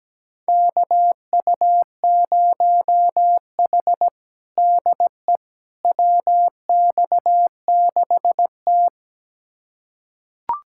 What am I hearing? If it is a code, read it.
Morse code, heard as a keyed tone, about 17 wpm, 700 Hz. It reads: KU0H DE WX6T
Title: USA Callsign DE Callsign - 4 Characters - 17wpm